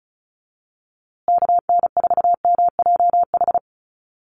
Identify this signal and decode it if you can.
Morse code, heard as a keyed tone, about 34 wpm, 700 Hz. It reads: KN4MJH
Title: USA Call Signs Repeated 2x 35wpm